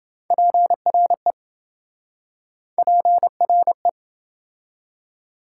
Morse code, heard as a keyed tone, about 28 wpm, 700 Hz. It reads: PRE PRE